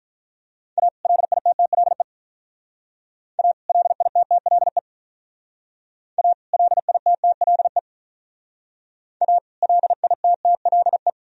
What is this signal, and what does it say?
Morse code, heard as a keyed tone, about 44 wpm, 700 Hz. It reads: A LITTLE A LITTLE A LITTLE A LITTLE